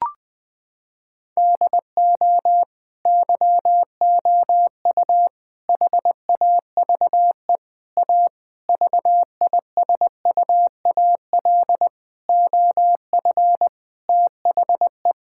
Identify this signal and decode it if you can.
Morse code, heard as a keyed tone, about 20 wpm, 700 Hz. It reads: DO YOU HAVE A VISUAL OF THE
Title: Sentences from Top 1000 Words - Review Part 6 - 20wpm